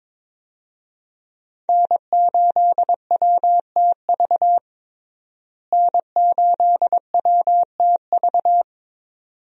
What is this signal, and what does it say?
Morse code, heard as a keyed tone, about 22 wpm, 700 Hz. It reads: N8WTV N8WTV
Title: USA Call Signs - 5 Characters - Repeated 2x - 22wpm